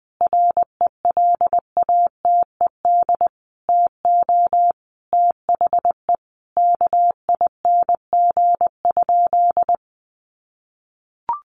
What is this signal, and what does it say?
Morse code, heard as a keyed tone, about 20 wpm, 700 Hz. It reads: RELATED TO THE KING?
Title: Sentences from Top 1400 Words 20wpm